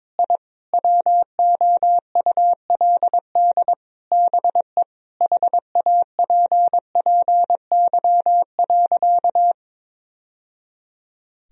Morse code, heard as a keyed tone, about 21 wpm, 700 Hz. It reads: I WOULD BE HAPPY.
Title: Sentences from Top 1000 Words - Review Part 3 - 22wpm